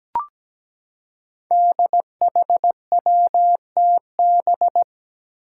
Morse code, heard as a keyed tone, about 17 wpm, 700 Hz. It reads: DHWTB
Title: Five Challenging Letter-Number Combinations 17wpm